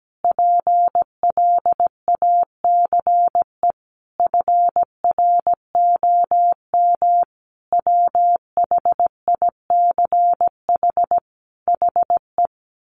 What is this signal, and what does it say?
Morse code, heard as a keyed tone, about 17 wpm, 700 Hz. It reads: PLACE FROM WHICH HE
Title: Sentences from Top 1500 Words 17wpm